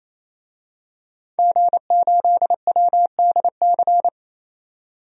Morse code, heard as a keyed tone, about 28 wpm, 700 Hz. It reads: G8WDC
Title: DXCC-100 Call Signs - 5 Characters - 28wpm